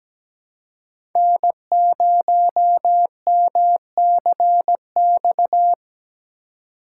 Morse code, heard as a keyed tone, about 17 wpm, 700 Hz. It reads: N0MCX
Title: USA Call Signs - 5 Characters - 3x Character Spacing - 17wpm